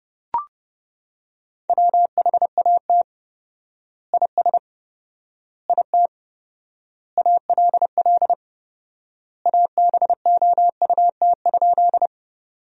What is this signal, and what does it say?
Morse code, heard as a keyed tone, about 30 wpm, 700 Hz. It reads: WHAT IS IT ALL ABOUT?
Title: 4x Word-Spacing - Sentences from Top 100 Words - 30wpm